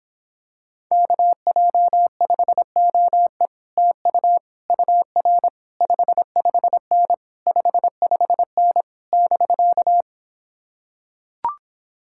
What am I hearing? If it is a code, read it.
Morse code, heard as a keyed tone, about 26 wpm, 700 Hz. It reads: KJ5OE TU UR 55N 55N <BK>